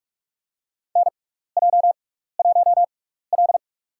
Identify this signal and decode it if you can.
Morse code, heard as a keyed tone, about 48 wpm, spaced as for 16 wpm, 700 Hz. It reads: NJ1L
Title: USA Call Signs - 4 Characters - 6x Character Spacing - 45wpm